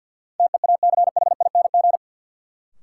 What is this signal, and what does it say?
Morse code, heard as a keyed tone, about 52 wpm, 700 Hz. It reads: TEACHING
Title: Sentences from Top 1100 Words - New Word Drill - 50wpm